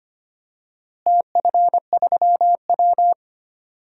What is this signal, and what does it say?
Morse code, heard as a keyed tone, about 25 wpm, 700 Hz. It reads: TF3W
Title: DXCC-100 Call Signs - 4 Characters - Repeated 3x - 25wpm